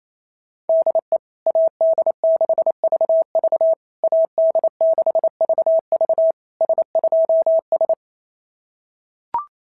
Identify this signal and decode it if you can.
Morse code, heard as a keyed tone, about 27 wpm, 650 Hz. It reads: DE AD6VV AD6VV S2S
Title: SOTA QSO Practice 28wpm